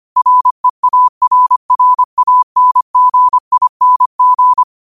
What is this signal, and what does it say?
Morse code, heard as a keyed tone, about 25 wpm, 1000 Hz. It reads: REARRANGING
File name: qotdfiles / 25wpm